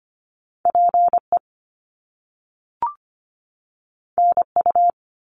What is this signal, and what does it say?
Morse code, heard as a keyed tone, about 25 wpm, 700 Hz. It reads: PE NU
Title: Canadian Provinces and Territories 25wpm